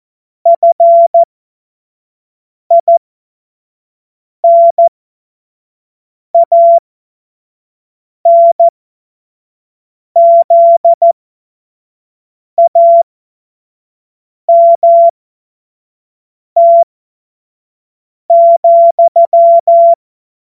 Morse code, heard as a keyed tone, about 14 wpm, spaced as for 5 wpm, 680 Hz. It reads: FINANZAMT,
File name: Morsekurs / Witze / Witz 03 Juni 2025 25 F70